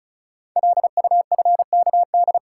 Morse code, heard as a keyed tone, about 35 wpm, 690 Hz. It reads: LUFKD
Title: Code Groups - 5 Letters - 35wpm